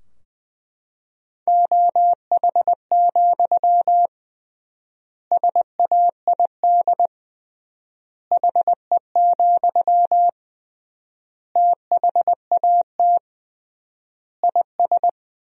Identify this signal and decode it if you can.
Morse code, heard as a keyed tone, about 21 wpm, 700 Hz. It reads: OH, SAID HE, THAT IS
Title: 3x Word-Spacing - Sentences from Top 500 Words - 20wpm